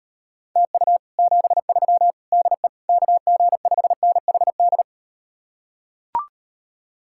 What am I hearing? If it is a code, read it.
Morse code, heard as a keyed tone, about 38 wpm, 700 Hz. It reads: TU 73 DE KG5NHD